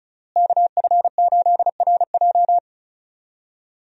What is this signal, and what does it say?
Morse code, heard as a keyed tone, about 34 wpm, 700 Hz. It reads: KF8RJ
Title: USA Call Signs - 5 Characters - Repeated 3x - 35wpm